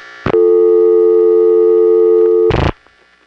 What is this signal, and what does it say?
Dial tone (North America), 350 + 440 Hz.